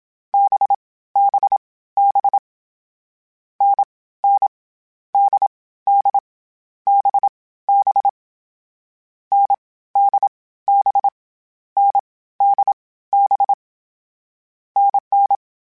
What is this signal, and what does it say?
Morse code, heard as a keyed tone, about 27 wpm, 800 Hz. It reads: B B B N N D D B B N D B N D B NN
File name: Difficult Combos / Difficult Combos-mp3s / NDB NDB6 @25